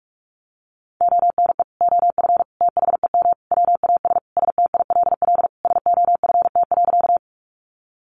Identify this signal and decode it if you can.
Morse code, heard as a keyed tone, about 44 wpm, 700 Hz. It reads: ONE OF THEM WAS STILL SOFT.
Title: Sentences from Top 1500 Words 45wpm